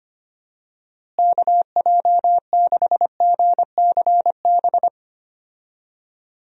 Morse code, heard as a keyed tone, about 25 wpm, 700 Hz. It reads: KJ6GCB
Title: USA Call Signs - 6 Characters - Repeated 2x - 25wpm